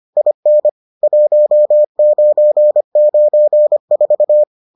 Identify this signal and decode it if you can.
Morse code, heard as a keyed tone, about 25 wpm, 600 Hz. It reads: IN 1994